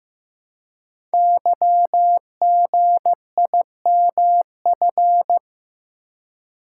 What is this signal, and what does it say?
Morse code, heard as a keyed tone, about 15 wpm, 700 Hz. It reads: YGIMF
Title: Code Groups - 5 Letter:Numbers - 15wpm